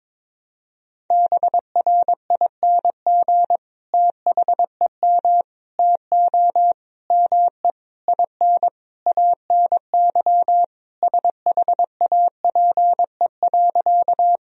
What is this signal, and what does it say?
Morse code, heard as a keyed tone, about 21 wpm, 700 Hz. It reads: BRING THEM TO ME IN ANY SHAPE.
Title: Sentences from Top 1000 Words - Review Part 3 - 22wpm